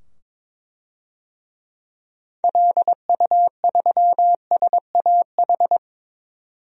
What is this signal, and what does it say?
Morse code, heard as a keyed tone, about 22 wpm, 700 Hz. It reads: LU3SAH
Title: DXCC-100 Call Signs - 6 Characters - 22wpm